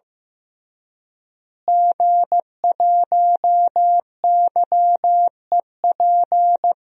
Morse code, heard as a keyed tone, about 15 wpm, 700 Hz. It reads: G1YEP
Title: DXCC-100 Call Signs - 5 Characters - Repeated 2x - 15wpm